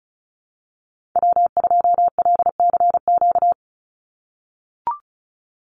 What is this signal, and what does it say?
Morse code, heard as a keyed tone, about 34 wpm, 700 Hz. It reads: W2LCQ
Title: Call Signs - 35wpm